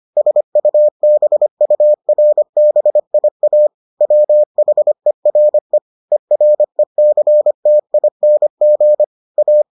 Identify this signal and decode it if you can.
Morse code, heard as a keyed tone, about 25 wpm, 600 Hz. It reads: SUBURBIA WHERE ERECTING A